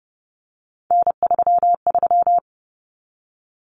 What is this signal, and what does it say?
Morse code, heard as a keyed tone, about 30 wpm, 700 Hz. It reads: N33
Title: Common Cut Numbers - 3-Digit 30wpm